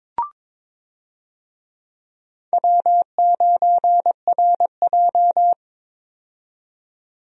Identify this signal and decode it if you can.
Morse code, heard as a keyed tone, about 22 wpm, 700 Hz. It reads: W9RJ
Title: USA Call Signs - 4 Characters - 22wpm